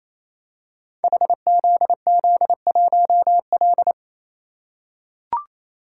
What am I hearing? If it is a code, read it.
Morse code, heard as a keyed tone, about 27 wpm, 700 Hz. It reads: HZZ1L
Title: Code Groups - 5 Letter:Numbers - 28wpm